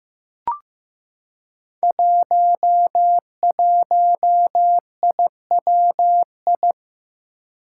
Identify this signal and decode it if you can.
Morse code, heard as a keyed tone, about 15 wpm, 700 Hz. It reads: 11IWI
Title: Five Challenging Letter-Number Combinations 15wpm